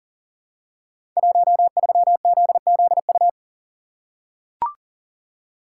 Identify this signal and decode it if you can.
Morse code, heard as a keyed tone, about 42 wpm, 700 Hz. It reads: 13ZZU
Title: Code Groups - 5 Letter:Numbers - 40wpm